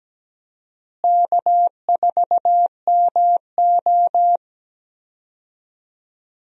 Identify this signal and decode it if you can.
Morse code, heard as a keyed tone, about 17 wpm, 700 Hz. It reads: K4MO
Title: USA Call Signs - 4 Characters - Repeated 3x - 17wpm